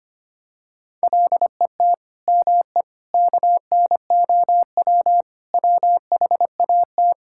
Morse code, heard as a keyed tone, about 25 wpm, 700 Hz. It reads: LET ME KNOW WHAT
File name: practice / 25wpm / Sentences with 9-Word Phrases - 25wpm